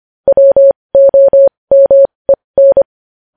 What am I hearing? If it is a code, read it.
Morse code, heard as a keyed tone, about 25 wpm, spaced as for 20 wpm, 550 Hz. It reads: WOMEN
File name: podcast / News headlines-1x-25x20